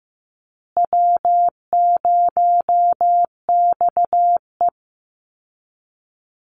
Morse code, heard as a keyed tone, about 15 wpm, 700 Hz. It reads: W0XE
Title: Easy Call Signs Repeated 3x - 15wpm